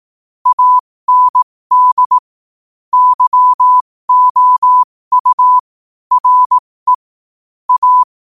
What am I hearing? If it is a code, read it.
Morse code, heard as a keyed tone, about 18 wpm, spaced as for 15 wpm, 1000 Hz. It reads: AND YOU RE A
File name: qotdfiles / 15wpm